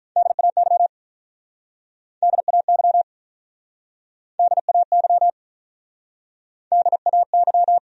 Morse code, heard as a keyed tone, about 47 wpm, 700 Hz. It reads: DAY DAY DAY DAY